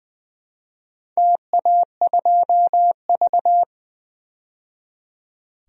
Morse code, heard as a keyed tone, about 20 wpm, 700 Hz. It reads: TA2V
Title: DXCC-100 Call Signs - 4 Characters - 20wpm